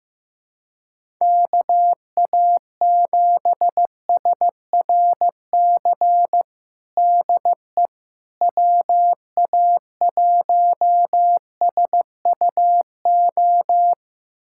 Morse code, heard as a keyed tone, about 16 wpm, 700 Hz. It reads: KA7SRC DE WA1SUO
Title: USA Callsign DE Callsign - 6 Characters - 15wpm